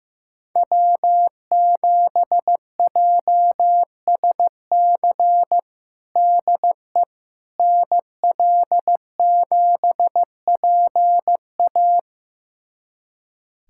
Morse code, heard as a keyed tone, about 15 wpm, 700 Hz. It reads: W7JSC DE NL7PA
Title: USA Callsign DE Callsign - 5 Characters - 15wpm